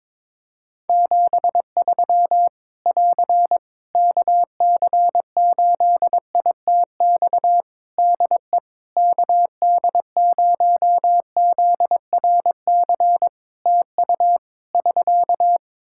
Morse code, heard as a keyed tone, about 22 wpm, 700 Hz. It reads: 73 <AR> KC8ITX DE KD0ZRC TU <SK>
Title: Practice QSO - Exchange 4 - Ending - 22wpm